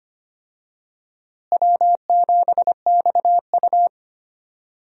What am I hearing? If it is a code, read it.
Morse code, heard as a keyed tone, about 25 wpm, 700 Hz. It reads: W7XU
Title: USA Call Signs - 4 Characters - Repeated 2x - 25wpm